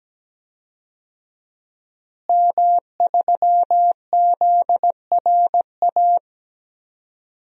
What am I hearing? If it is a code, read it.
Morse code, heard as a keyed tone, about 17 wpm, 700 Hz. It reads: M3ZRA
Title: DXCC-100 Call Signs - 5 Characters - Repeated 2x - 17wpm